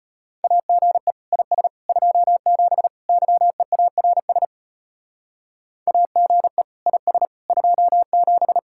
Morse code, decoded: AGE IS 27 YEARS AGE IS 27

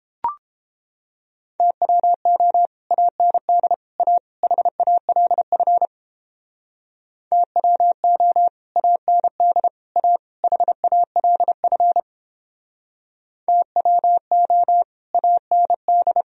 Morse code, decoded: TWO AND A HALF TWO AND A HALF TWO AND

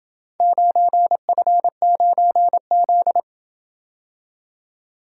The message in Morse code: 9F9Z

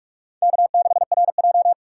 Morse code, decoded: K6RJ

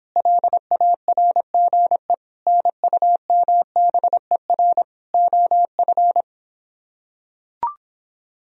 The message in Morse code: LARGE NUMBER OF

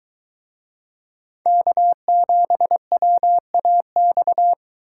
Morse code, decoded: K7WAX

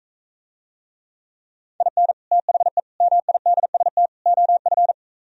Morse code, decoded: IN THE MIDST OF

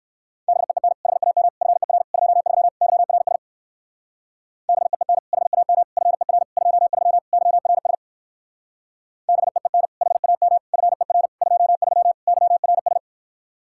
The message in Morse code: BEEN HAM FER 23 YRS BEEN HAM FER 23 YRS BEEN HAM FER 23 YRS

